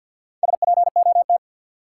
Morse code, heard as a keyed tone, about 48 wpm, 700 Hz. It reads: SPOT
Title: Sentences from Top 1100 Words - New Word Drill Alt - 50wpm